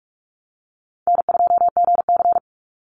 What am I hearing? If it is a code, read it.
Morse code, heard as a keyed tone, about 46 wpm, 700 Hz. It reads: N2GC
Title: Easy Call Signs - 45wpm